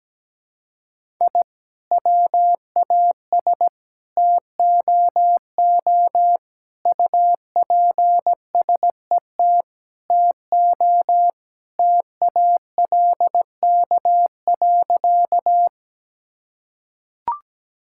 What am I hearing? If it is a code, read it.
Morse code, heard as a keyed tone, about 17 wpm, 700 Hz. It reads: I WAS TOO UPSET TO TALK.
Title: Sentences from Top 1000 Words - Review Part 5 - 17wpm